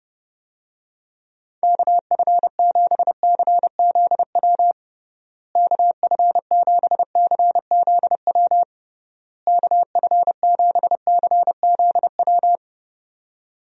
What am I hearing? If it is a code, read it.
Morse code, heard as a keyed tone, about 30 wpm, 700 Hz. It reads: KF7CZW KF7CZW KF7CZW
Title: USA Call Signs - 6 Characters - Repeated 3x - 30wpm